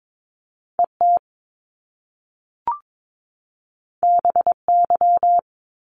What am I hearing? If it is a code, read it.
Morse code, heard as a keyed tone, about 22 wpm, 700 Hz. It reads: ET BY
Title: Top Two Letter Combinations in Words 22wpm